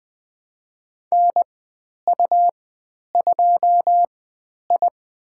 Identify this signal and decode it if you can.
Morse code, heard as a keyed tone, about 20 wpm, spaced as for 10 wpm, 700 Hz. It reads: NU2I
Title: 109 Lesson 18g - Letter m - Call Signs 20wpm